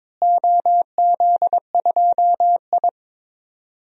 Morse code, heard as a keyed tone, about 22 wpm, 700 Hz. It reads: OZ2I
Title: DXCC-100 Call Signs - 4 Characters - Repeated 3x - 22wpm